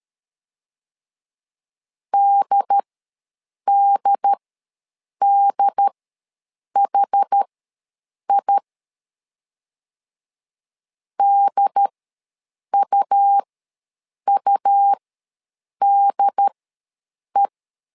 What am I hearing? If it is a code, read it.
Morse code, heard as a keyed tone, about 13 wpm, spaced as for 7 wpm, 800 Hz. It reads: DDDHI DUUDE
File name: LEZIONI / 2 UHDN / 2 UHD ws 12wpm 1722